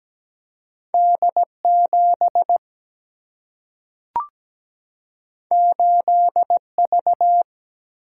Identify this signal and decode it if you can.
Morse code, heard as a keyed tone, about 17 wpm, 700 Hz. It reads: D 7 8 V
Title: Single-Letter Single-Number 17wpm